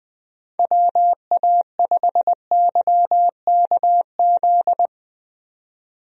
Morse code, heard as a keyed tone, about 21 wpm, 700 Hz. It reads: WA5YKZ
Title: USA Call Signs - 6 Characters - 20wpm